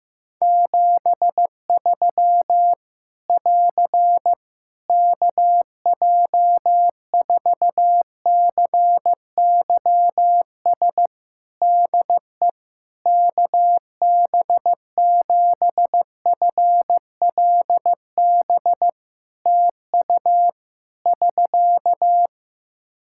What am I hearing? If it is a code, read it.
Morse code, heard as a keyed tone, about 15 wpm, 700 Hz. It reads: 73 <AR> KJ4CYS DE KB7FLB TU <SK>
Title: Practice QSO - Exchange 4 - Ending - 15wpm